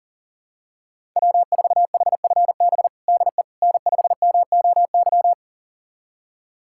Morse code, heard as a keyed tone, about 40 wpm, 700 Hz. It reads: W4HFB DE N5MOY
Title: USA Callsign DE Callsign - 5 Characters - 40wpm